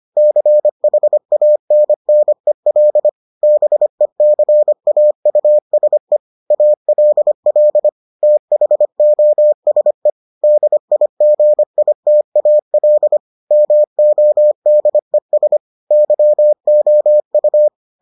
Morse code, heard as a keyed tone, about 25 wpm, 600 Hz. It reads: CHANNEL BECAUSE ALL THOSE DIGITAL MODES YOU